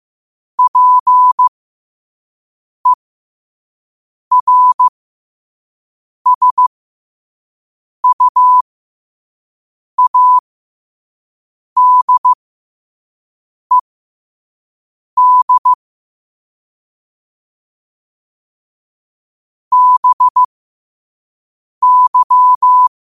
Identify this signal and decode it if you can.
Morse code, heard as a keyed tone, about 15 wpm, spaced as for 5 wpm, 1000 Hz. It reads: PERSUADED BY
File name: qotdfiles / 5wpm